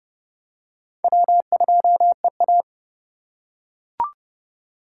Morse code, heard as a keyed tone, about 29 wpm, 700 Hz. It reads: W2EA